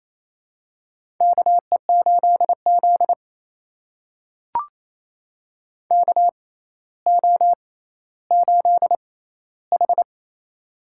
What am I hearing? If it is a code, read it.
Morse code, heard as a keyed tone, about 28 wpm, spaced as for 14 wpm, 700 Hz. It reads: KE8Z KO8H
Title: USA Call Signs - 4 Characters - 6x Character Spacing - 28wpm